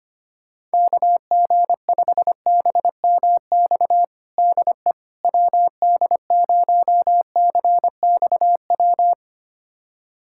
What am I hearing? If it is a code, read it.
Morse code, heard as a keyed tone, about 25 wpm, 700 Hz. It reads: KG5BMX DE WD0CXW
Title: USA Callsign DE Callsign - 6 Characters - 25wpm